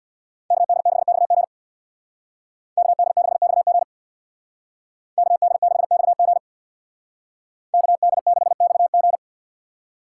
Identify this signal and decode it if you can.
Morse code, heard as a keyed tone, about 61 wpm, 690 Hz. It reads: KD6XG KD6XG KD6XG KD6XG
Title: Speed Racing - USA 2x2 Call Signs 50wpm